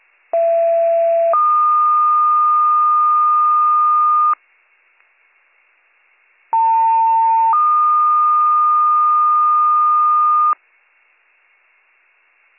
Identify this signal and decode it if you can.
Two-tone page, heard as a one-second tone, then a three-second tone, about 670 then 1220 Hz, about 900 then 1220 Hz. It is heard from 0.3 to 10.5 s.